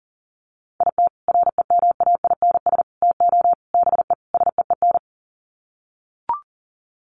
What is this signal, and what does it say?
Morse code, heard as a keyed tone, about 39 wpm, 700 Hz. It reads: IT REMAINS TO BE SEEN
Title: Sets of 5 Words Encore 40wpm